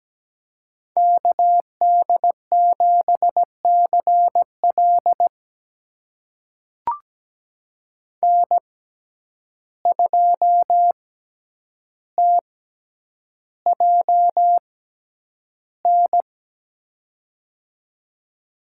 Morse code, heard as a keyed tone, about 17 wpm, spaced as for 9 wpm, 700 Hz. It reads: KD7CL N2TJN
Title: USA Call Signs - 5 Characters - 6x Character Spacing - 17wpm